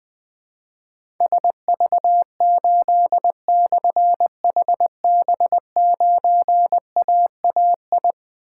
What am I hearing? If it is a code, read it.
Morse code, heard as a keyed tone, about 20 wpm, 700 Hz. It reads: SV8/HB9AAI
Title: Difficult Call Signs 20wpm